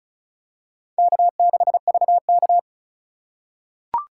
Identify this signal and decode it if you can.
Morse code, heard as a keyed tone, about 34 wpm, 700 Hz. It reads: K6VK
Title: USA Call Signs - 4 Characters - 3x Character Spacing - 35wpm